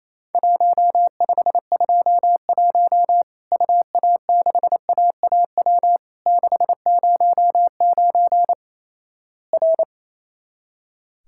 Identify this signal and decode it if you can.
Morse code, heard as a keyed tone, about 28 wpm, 700 Hz. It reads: 1521 UA6AAW 609 R
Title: Worked All Europe DX Contest - QTC Exchange - 28wpm